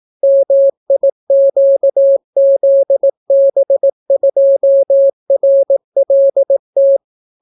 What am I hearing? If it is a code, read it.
Morse code, heard as a keyed tone, about 18 wpm, 550 Hz. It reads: MIQZB2RLT